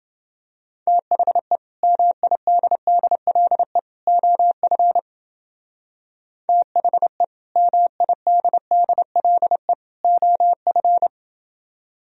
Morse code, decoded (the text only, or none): THE MIDDLE OF THE MIDDLE OF